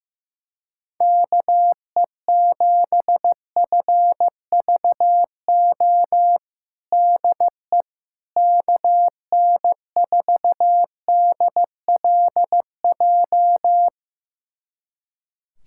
Morse code, heard as a keyed tone, about 15 wpm, 700 Hz. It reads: KE7FVO DE KN4DLJ
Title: USA Callsign DE Callsign - 6 Characters - 15wpm